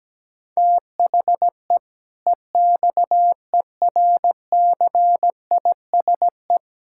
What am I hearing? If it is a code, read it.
Morse code, heard as a keyed tone, about 17 wpm, 700 Hz. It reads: THE EXERCISE
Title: Sentences from Top 1000 Words - Review Part 6 - 17wpm